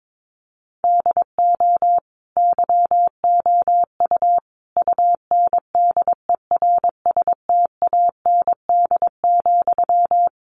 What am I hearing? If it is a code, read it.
Morse code, heard as a keyed tone, about 22 wpm, 700 Hz. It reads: DO YOU UNDERSTAND,
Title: Sentences from Top 1200 Words 22wpm